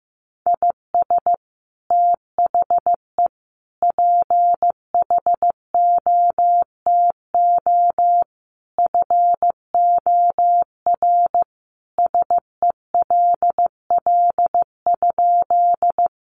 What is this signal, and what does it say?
Morse code, heard as a keyed tone, about 15 wpm, 700 Hz. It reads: IS THE PHOTO FOR SELL?
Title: Sentences from Top 1200 Words 15wpm